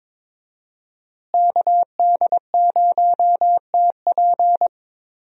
Morse code, heard as a keyed tone, about 22 wpm, 700 Hz. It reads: KD0TP
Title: USA Call Signs - 5 Characters - Repeated 3x - 22wpm